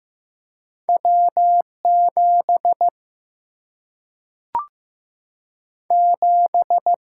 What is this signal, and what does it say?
Morse code, heard as a keyed tone, about 15 wpm, 700 Hz. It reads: W 7 7
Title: Single-Letter Single-Number 15wpm